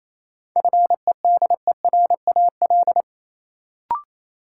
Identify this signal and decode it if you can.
Morse code, heard as a keyed tone, about 28 wpm, 700 Hz. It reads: FEDERAL
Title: Sentences from Top 1000 Words - Review Part 5 - New Word Drill - 28wpm